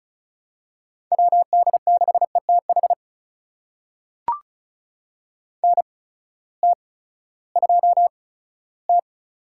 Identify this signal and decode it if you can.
Morse code, heard as a keyed tone, about 36 wpm, spaced as for 18 wpm, 700 Hz. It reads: WD6ETH NT2T
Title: USA Call Signs - 8x Character Spacing - 35wpm